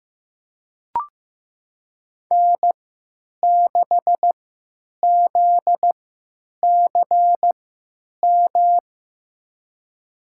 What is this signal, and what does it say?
Morse code, heard as a keyed tone, about 15 wpm, spaced as for 9 wpm, 700 Hz. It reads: N6ZCM
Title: USA Call Signs - 5 Characters - 3x Character Spacing - 15wpm